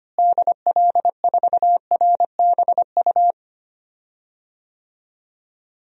Morse code, heard as a keyed tone, about 25 wpm, 700 Hz. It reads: DL4RBU
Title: DXCC-100 Call Signs - 6 Characters - Repeated 2x - 25wpm